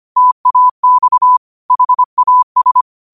Morse code, heard as a keyed tone, about 25 wpm, 1000 Hz. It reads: TAX HAS